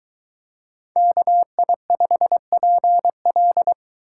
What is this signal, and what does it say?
Morse code, heard as a keyed tone, about 22 wpm, 700 Hz. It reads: KI5PL